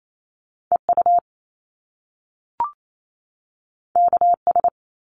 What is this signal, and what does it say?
Morse code, heard as a keyed tone, about 28 wpm, 700 Hz. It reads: EU KS
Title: Top Two Letter Combinations in Words 28wpm